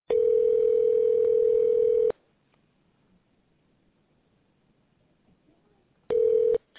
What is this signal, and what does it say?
Ringback tone (North America), 440 + 480 Hz.